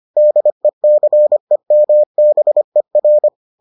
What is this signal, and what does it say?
Morse code, heard as a keyed tone, about 25 wpm, 600 Hz. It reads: DECEMBER